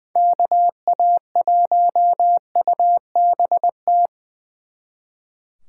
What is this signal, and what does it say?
Morse code, heard as a keyed tone, about 23 wpm, 700 Hz. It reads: KA1UBT
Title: Speed Racing - USA 2x3 Call Signs 20wpm